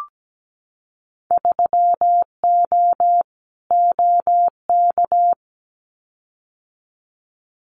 Morse code, heard as a keyed tone, about 17 wpm, 700 Hz. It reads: 3O OK
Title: Winter Field Day Exchanges 17wpm